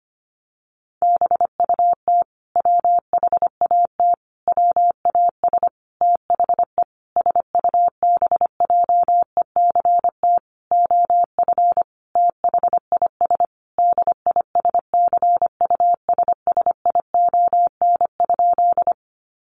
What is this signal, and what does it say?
Morse code, heard as a keyed tone, about 25 wpm, 700 Hz. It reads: BUT WHAT WAS THE SUBJECT OF THIS DISCUSSION?
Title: Sentences from Top 1100 Words 25wpm